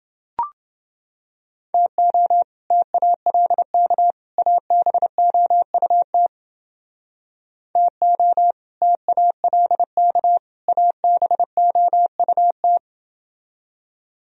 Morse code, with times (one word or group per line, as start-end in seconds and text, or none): 1.7-2.4 TO
2.7-4.1 TALK
4.4-6.3 ABOUT
7.8-8.5 TO
8.8-10.4 TALK
10.7-12.8 ABOUT